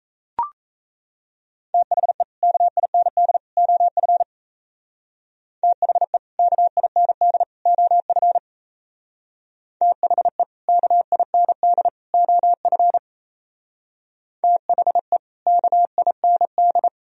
Morse code, heard as a keyed tone, about 35 wpm, 700 Hz. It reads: THE KIND OF THE KIND OF THE KIND OF THE KIND